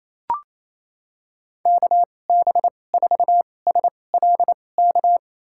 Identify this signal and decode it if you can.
Morse code, heard as a keyed tone, about 28 wpm, spaced as for 20 wpm, 700 Hz. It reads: KB4SLK
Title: USA Call Signs - 6 Characters - 2x Character Spacing - 28wpm